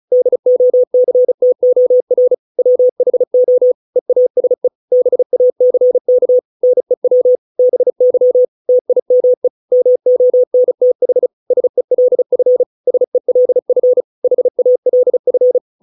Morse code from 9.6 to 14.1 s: MONTH SELF SELF